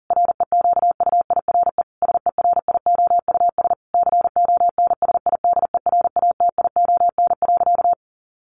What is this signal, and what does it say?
Morse code, heard as a keyed tone, about 40 wpm, 700 Hz. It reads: REQUIRE SERIOUS CONSIDERATION.
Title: Sentences from Top 1500 Words 40wpm